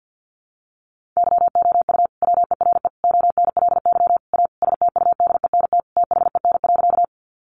Morse code, heard as a keyed tone, about 50 wpm, 700 Hz. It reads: YOU WERE ONLY A STUDENT THEN.
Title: Sentences from Top 1000 Words 50wpm